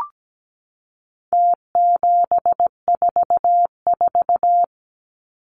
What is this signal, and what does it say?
Morse code, heard as a keyed tone, about 17 wpm, 700 Hz. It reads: T744
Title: Numbers - Contest Style - 4-Digit 17wpm